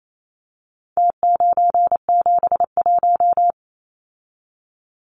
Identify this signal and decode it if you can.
Morse code, heard as a keyed tone, about 28 wpm, 700 Hz. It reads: T971
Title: Numbers - Contest Style - 4-Digit 28wpm